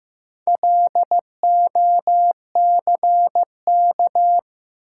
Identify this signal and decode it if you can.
Morse code, heard as a keyed tone, about 15 wpm, 700 Hz. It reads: LOCK
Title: Sentences from Top 1300 Words - New Word Drill Alt - 15wpm